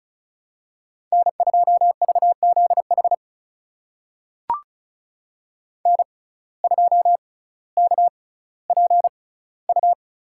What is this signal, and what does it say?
Morse code, heard as a keyed tone, about 36 wpm, spaced as for 18 wpm, 700 Hz. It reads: N2VZH N2KPU
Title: USA Call Signs - 6x Character Spacing - 35wpm